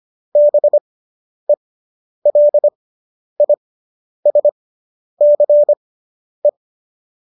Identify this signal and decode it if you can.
Morse code, heard as a keyed tone, about 25 wpm, spaced as for 10 wpm, 590 Hz. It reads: BELISCE
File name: qso219 10